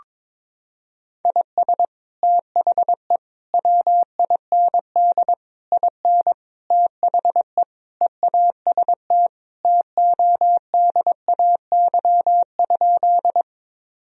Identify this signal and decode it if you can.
Morse code, heard as a keyed tone, about 21 wpm, 700 Hz. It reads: IS THE WIND IN THE EAST TODAY?
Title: Sentences from Top 1000 Words - Review Part 3 - 22wpm